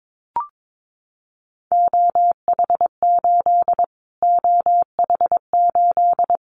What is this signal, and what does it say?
Morse code, heard as a keyed tone, about 22 wpm, 700 Hz. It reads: OH8 OH8
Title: Callsign Prefixes for Top 25 DXCC - Repeated 3x - 22wpm